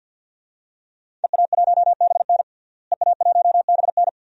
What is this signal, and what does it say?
Morse code, heard as a keyed tone, about 49 wpm, 700 Hz. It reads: EA1BN EA1BN
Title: DXCC-100 Call Signs - 5 Characters - Repeated 2x - 50wpm